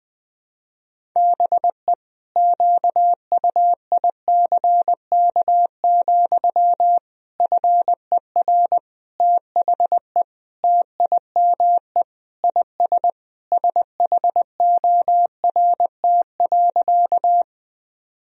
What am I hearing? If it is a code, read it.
Morse code, heard as a keyed tone, about 20 wpm, 700 Hz. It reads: BE QUICK, FER THE TIME IS SHORT.
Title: Sentences from Top 500 Words and Abbreviations 20wpm